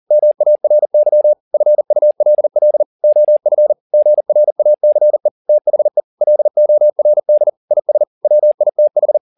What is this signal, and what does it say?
Morse code, heard as a keyed tone, about 40 wpm, 600 Hz. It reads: MARY FULL OF GRACE THE LORD IS WITH